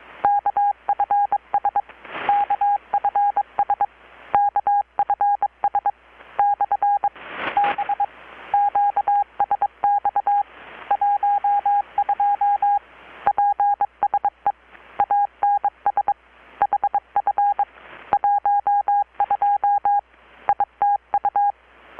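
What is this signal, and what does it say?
Morse code, heard as a keyed tone, about 22 wpm, 810 Hz. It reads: KFS KFS KFS/B QSX 12 PSE ANS HF 12 ITU
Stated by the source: Morse from KPH's Night of Nights XX maritime radiotelegraphy event, received in Chiapas, Mexico about 3,800 km away on a Kenwood R-600 with a grounded 10 m long-wire antenna.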